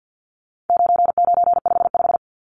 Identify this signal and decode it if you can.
Morse code, heard as a keyed tone, about 50 wpm, 700 Hz. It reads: 9955
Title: Numbers - 4-Digit 50wpm